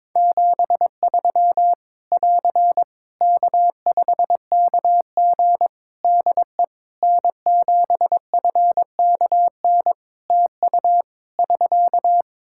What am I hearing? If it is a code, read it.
Morse code, heard as a keyed tone, about 22 wpm, 700 Hz. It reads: 73 <AR> K5KG DE N7FKN TU <SK>
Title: K1USN SST QSO Practice 22wpm